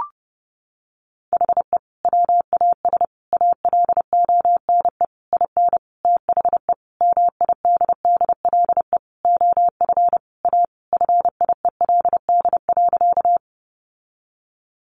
Morse code, heard as a keyed tone, about 30 wpm, 700 Hz. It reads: HE WAS ALONE IN THE MIDDLE OF A FIELD.
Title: Sentences from Top 1000 Words 30wpm